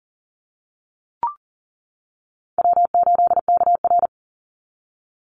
Morse code, heard as a keyed tone, about 38 wpm, 700 Hz. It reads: W8KR
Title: Easy Call Signs - 40wpm